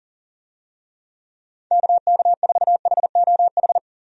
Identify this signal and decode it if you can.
Morse code, heard as a keyed tone, about 40 wpm, 700 Hz. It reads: KK4HOH
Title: USA Call Signs - 6 Characters - 40wpm